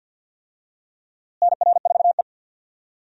Morse code, heard as a keyed tone, about 52 wpm, 700 Hz. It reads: NR4E
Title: USA Call Signs - 4 Characters - Repeated 3x - 50wpm